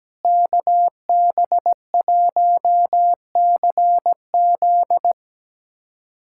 Morse code, heard as a keyed tone, about 17 wpm, 700 Hz. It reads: KB1CZ
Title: USA Callsign DE Callsign - 5 Characters - 17wpm